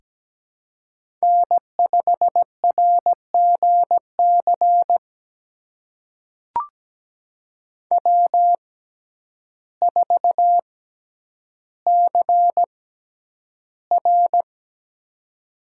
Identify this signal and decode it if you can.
Morse code, heard as a keyed tone, about 17 wpm, spaced as for 9 wpm, 700 Hz. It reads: N5RGC W4CR
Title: USA Call Signs - 6x Character Spacing - 17wpm